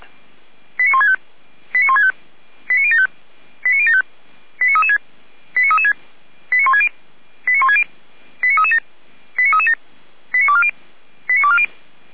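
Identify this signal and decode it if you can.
Five-tone selective call (ZVEI), heard as quick runs of five tones, tones of about 70 ms. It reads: 89165 89165 89075 89075 89336 89336 89160 89160 89337 89337 89240 89240
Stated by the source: ZVEI five-tone selective call received on an SDR, from feuerwehr-orlamuende.org (Orlamünde fire brigade).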